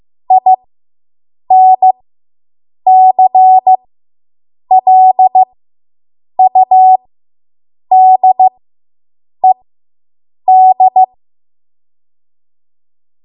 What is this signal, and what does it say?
Morse code, heard as a keyed tone, about 15 wpm, spaced as for 7 wpm, 750 Hz. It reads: INCLUDED